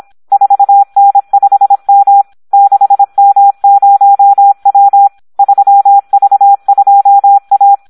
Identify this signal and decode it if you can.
Morse code, heard as a keyed tone, about 26 wpm, 800 Hz. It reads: 4N5M 6M0W 3V2A
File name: Audio practica CW / SD 1 26